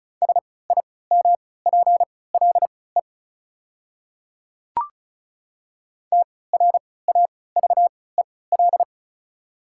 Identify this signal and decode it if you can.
Morse code, heard as a keyed tone, about 36 wpm, spaced as for 20 wpm, 690 Hz. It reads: SIMPLE TRAVEL
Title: 3x Character Spacing - Top 500 Words - 35wpm